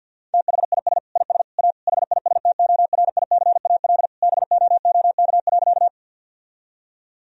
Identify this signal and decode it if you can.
Morse code, heard as a keyed tone, about 50 wpm, 700 Hz. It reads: THIS IS A HISTORICAL BOOK.
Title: Sentences from Top 1000 Words - Review Part 4 - 50wpm